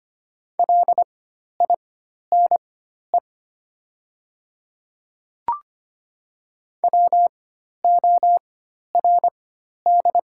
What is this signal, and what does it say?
Morse code, heard as a keyed tone, about 25 wpm, spaced as for 12 wpm, 700 Hz. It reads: LINE WORD
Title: 4x Character Spacing - Top 300 Words - 25wpm